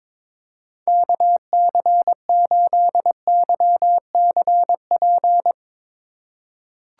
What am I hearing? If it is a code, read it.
Morse code, heard as a keyed tone, about 22 wpm, 700 Hz. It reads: KC8YCP
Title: USA Call Signs - 6 Characters - 2x Character Spacing - 22wpm